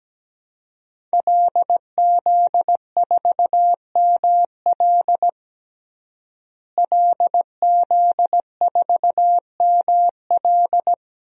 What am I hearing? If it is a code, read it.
Morse code, heard as a keyed tone, about 17 wpm, 700 Hz. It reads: LZ4ML LZ4ML
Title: DXCC-100 Call Signs - 5 Characters - Repeated 2x - 17wpm